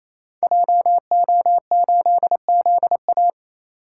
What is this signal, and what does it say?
Morse code, heard as a keyed tone, about 28 wpm, 700 Hz. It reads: JO8ZA
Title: Code Groups - 5 Letter:Numbers - 28wpm